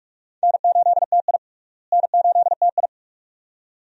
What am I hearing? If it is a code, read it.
Morse code, heard as a keyed tone, about 41 wpm, 700 Hz. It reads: N8TI N8TI
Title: USA Call Signs - 4 Characters - Repeated 2x - 45wpm